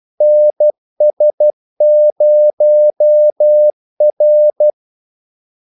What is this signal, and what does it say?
Morse code, heard as a keyed tone, about 12 wpm, 600 Hz. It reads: NS0R
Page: 2 ekstra ordmellemrum mellem kaldesignalerne